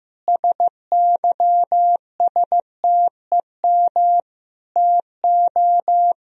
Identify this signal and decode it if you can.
Morse code, heard as a keyed tone, about 15 wpm, 700 Hz. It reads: SYSTEM TO
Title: Sentences from Top 1000 Words - Review Part 3 - 15wpm